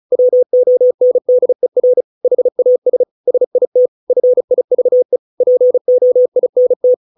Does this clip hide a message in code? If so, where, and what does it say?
Yes: Morse code from 0.1 to 6.9 s, heard as a keyed tone, about 34 wpm, 500 Hz. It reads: WONDER HAS SIT FIVE POINT